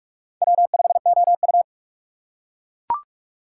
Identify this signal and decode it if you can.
Morse code, heard as a keyed tone, about 43 wpm, 700 Hz. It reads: W5OU
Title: USA Call Signs - 4 Characters - 6x Character Spacing - 45wpm